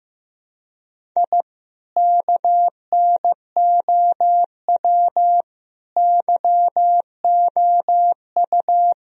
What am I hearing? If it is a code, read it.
Morse code, heard as a keyed tone, about 15 wpm, 700 Hz. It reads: I KNOW YOU
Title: Sentences from Top 1000 Words - Review Part 1 - 15wpm